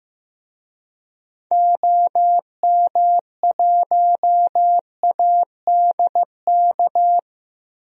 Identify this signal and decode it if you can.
Morse code, heard as a keyed tone, about 15 wpm, 690 Hz. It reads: OM1ADK
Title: DXCC-100 Call Signs - 6 Characters - 15wpm